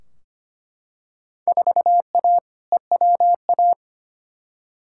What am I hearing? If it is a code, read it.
Morse code, heard as a keyed tone, about 25 wpm, 700 Hz. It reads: 4A EWA
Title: Field Day Exchanges - 25wpm